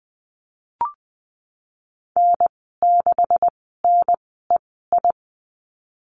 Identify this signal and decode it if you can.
Morse code, heard as a keyed tone, about 20 wpm, spaced as for 15 wpm, 700 Hz. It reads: N6NEI
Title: 140 Lesson 23g - Letter f - Call Signs 20wpm